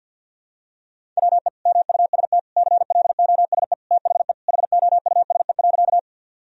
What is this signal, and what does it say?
Morse code, heard as a keyed tone, about 52 wpm, 700 Hz. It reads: WE MUST CLOSE THE HOUSE.
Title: Sentences from Top 1000 Words - Review Part 1 - 50wpm